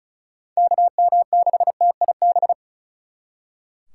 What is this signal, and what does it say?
Morse code, heard as a keyed tone, about 45 wpm, 700 Hz. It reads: KM6TIB